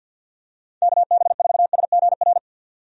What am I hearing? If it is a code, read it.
Morse code, heard as a keyed tone, about 46 wpm, 700 Hz. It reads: KB4SGR
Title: USA Call Signs - 6 Characters - 50wpm